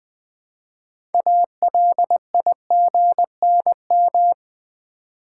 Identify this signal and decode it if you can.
Morse code, heard as a keyed tone, about 20 wpm, 700 Hz. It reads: ALIGNM
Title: Build A Word - 3 Letters At A Time - 9-Letter Words - 20wpm